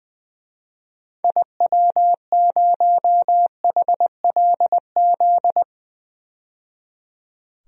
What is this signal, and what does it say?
Morse code, heard as a keyed tone, about 21 wpm, 700 Hz. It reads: IW0HLZ
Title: DXCC-100 Call Signs - 6 Characters - 20wpm